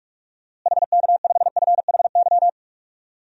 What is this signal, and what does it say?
Morse code, heard as a keyed tone, about 46 wpm, 700 Hz. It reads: HK5FHY
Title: DXCC-100 Call Signs - 6 Characters - Repeated 2x - 45wpm